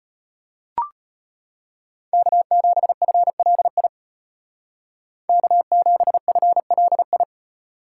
Morse code, heard as a keyed tone, about 33 wpm, 700 Hz. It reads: K7FLI K7FLI